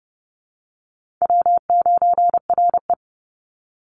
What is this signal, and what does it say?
Morse code, heard as a keyed tone, about 30 wpm, 700 Hz. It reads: W9RE
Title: Easy Call Signs Repeated 2x - 30wpm